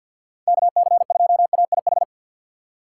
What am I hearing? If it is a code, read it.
Morse code, heard as a keyed tone, about 53 wpm, 700 Hz. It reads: KC2AIH
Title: USA Call Signs - 4x Character Spacing - 50wpm